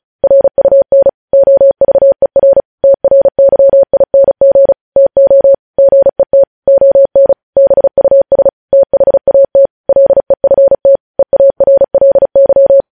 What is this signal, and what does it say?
Morse code, heard as a keyed tone, about 36 wpm, 550 Hz. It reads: RUN OVER TRYING TO GET ON BUS THAT LEFT EARLY
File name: podcast / News headlines-1x-35x35